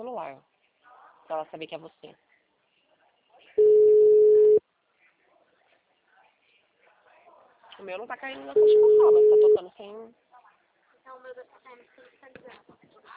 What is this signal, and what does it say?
Ringback tone (Europe, 425 Hz), with speech.
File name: out- -2020- - - .178